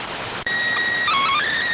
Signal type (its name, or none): SSTV (slow-scan television)